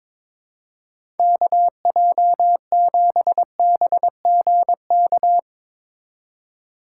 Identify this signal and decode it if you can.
Morse code, heard as a keyed tone, about 22 wpm, 700 Hz. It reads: KJ7BGK
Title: USA Call Signs - 6 Characters - 2x Character Spacing - 22wpm